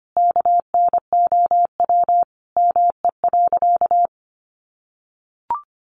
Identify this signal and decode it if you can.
Morse code, heard as a keyed tone, about 25 wpm, 700 Hz. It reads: KNOW ME.
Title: Sentences from Top 600 Words 25wpm